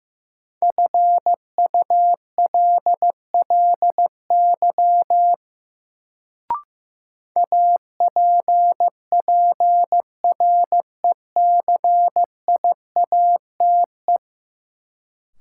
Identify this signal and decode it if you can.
Morse code, heard as a keyed tone, about 15 wpm, 700 Hz. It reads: FULLY APPRECIATE
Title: Sentences from Top 1300 Words - New Word Drill - 15wpm